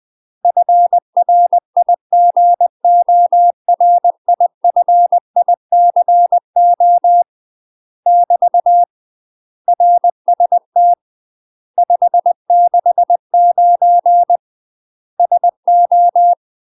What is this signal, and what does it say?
Morse code, heard as a keyed tone, about 20 wpm, 700 Hz. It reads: FRIGORIFICO <BT> RST 569 SO
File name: UK MASTER QSOs TRAINING / 20WPM / UK MASTER QSOs 0032